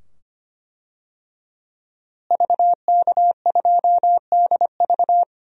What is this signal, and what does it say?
Morse code, heard as a keyed tone, about 25 wpm, 700 Hz. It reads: VK2DV
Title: DXCC-100 Call Signs - 5 Characters - Repeated 2x - 25wpm